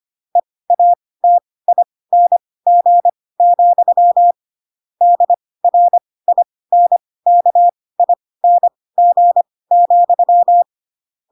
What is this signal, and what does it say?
Morse code, heard as a keyed tone, about 25 wpm, spaced as for 18 wpm, 700 Hz. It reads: EATING, DRINKING,
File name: Pinocchio / Pinocchio ch4 12 25 ik1hge